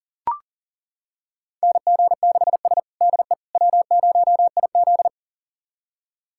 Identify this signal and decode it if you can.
Morse code, heard as a keyed tone, about 37 wpm, 700 Hz. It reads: NG6S DE W0IZ
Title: USA Callsign DE Callsign - 4 Characters - 40wpm